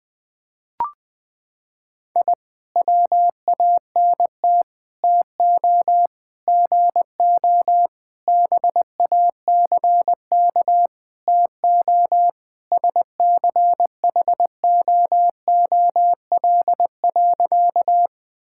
Morse code, heard as a keyed tone, about 20 wpm, 700 Hz. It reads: I WANT TO GO BACK TO SCHOOL.
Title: Sentences with 4-Word Phrases - 20wpm